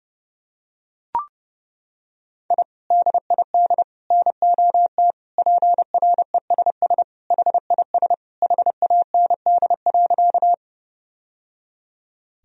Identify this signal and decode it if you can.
Morse code, heard as a keyed tone, about 30 wpm, 700 Hz. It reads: I DID NOT PRESS HIS HAND.
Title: Sentences from Top 1000 Words - Review Part 1 - 30wpm